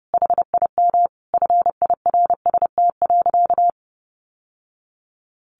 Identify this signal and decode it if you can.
Morse code, heard as a keyed tone, about 30 wpm, 700 Hz. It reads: HIM FIRST.
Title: Sentences from Top 1100 Words 30wpm